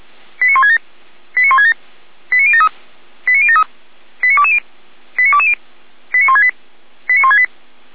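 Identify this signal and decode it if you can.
Five-tone selective call (ZVEI), heard as quick runs of five tones, tones of about 70 ms. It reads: 89167 89167 89073 89073 89220 89220 89168 89168
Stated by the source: ZVEI five-tone selective call, code 89001, received via SDR.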